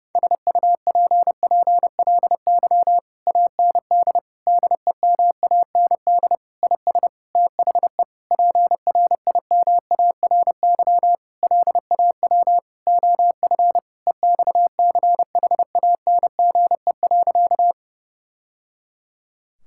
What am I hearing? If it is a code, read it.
Morse code, heard as a keyed tone, about 32 wpm, 700 Hz. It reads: SUPPLY AND DEMAND IS THE PRIMARY LAW OF EXCHANGE.
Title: Sentences from Top 1000 Words - Review Part 6 - 30wpm